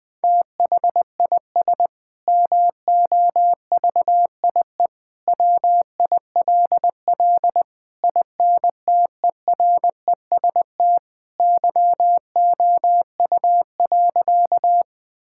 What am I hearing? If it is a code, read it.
Morse code, heard as a keyed tone, about 20 wpm, 700 Hz. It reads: THIS MOVIE WILL INTEREST YOU.
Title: Sentences from Top 1000 Words - Review Part 5 - 20wpm